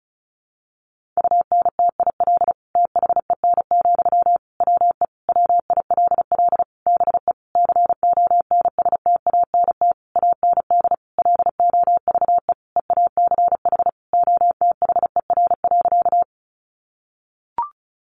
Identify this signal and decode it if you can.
Morse code, heard as a keyed tone, about 34 wpm, 700 Hz. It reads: UNTIL THEN, WE WILL BE CONSTANT AND LOVE EACH OTHER.
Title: Sentences from Top 1500 Words 35wpm